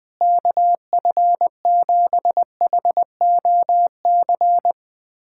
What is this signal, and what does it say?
Morse code, heard as a keyed tone, about 20 wpm, 700 Hz. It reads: KF7HOC
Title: USA Callsign DE Callsign 20wpm